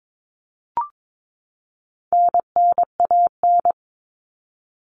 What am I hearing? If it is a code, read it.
Morse code, heard as a keyed tone, about 22 wpm, 700 Hz. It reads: NNAN
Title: All Cut Numbers - 4-Digit 22wpm